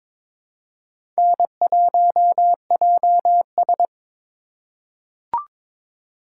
Morse code, heard as a keyed tone, about 22 wpm, 700 Hz. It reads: N1JS
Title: USA Call Signs - 4 Characters - 4x Character Spacing - 22wpm